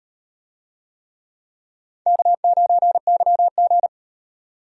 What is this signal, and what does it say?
Morse code, heard as a keyed tone, about 36 wpm, 700 Hz. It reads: K9YG